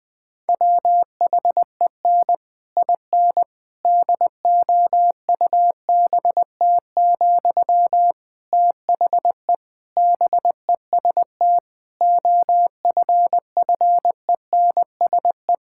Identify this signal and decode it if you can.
Morse code, heard as a keyed tone, about 20 wpm, 700 Hz. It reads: WHEN IN DOUBT, THE BEST OFFENSE
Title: Difficult Sentences with Hints - 20wpm